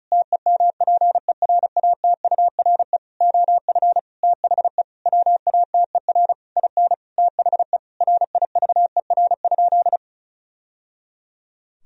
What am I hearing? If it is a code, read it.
Morse code, heard as a keyed tone, about 36 wpm, 700 Hz. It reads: TEMPERATURE OF THE WATER IN THE RIVER?
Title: Sentences from Top 1000 Words - Review Part 4 - 35wpm